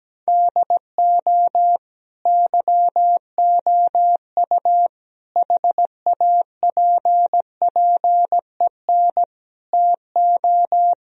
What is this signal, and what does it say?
Morse code, heard as a keyed tone, about 17 wpm, 700 Hz. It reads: DO YOU HAPPEN TO
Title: Sentences from Top 1000 Words - Review Part 2 - 17wpm